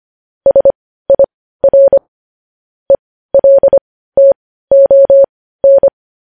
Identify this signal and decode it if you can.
Morse code, heard as a keyed tone, about 25 wpm, spaced as for 15 wpm, 550 Hz. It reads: SIR ELTON
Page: news-headlines-in-morse-code-at-15-wpm